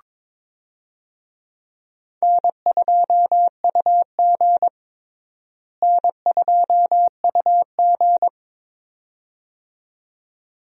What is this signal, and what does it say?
Morse code, heard as a keyed tone, about 22 wpm, 700 Hz. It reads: N2UG N2UG
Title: USA Call Signs - 4 Characters - Repeated 2x - 22wpm